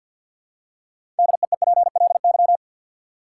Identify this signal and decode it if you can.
Morse code, heard as a keyed tone, about 53 wpm, 700 Hz. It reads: DEEPLY
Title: Six Letter Words - 50wpm